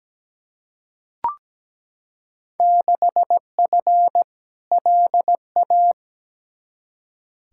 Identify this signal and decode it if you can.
Morse code, heard as a keyed tone, about 17 wpm, 700 Hz. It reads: 6F LA
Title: Field Day Exchanges Encore - 17wpm